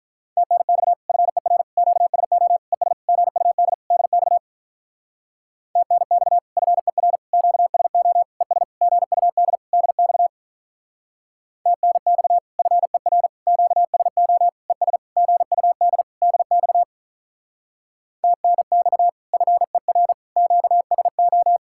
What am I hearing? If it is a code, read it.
Morse code, heard as a keyed tone, about 47 wpm, 700 Hz. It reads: TNX FER QSO ES GUD DX TNX FER QSO ES GUD DX TNX FER QSO ES GUD DX TNX FER QSO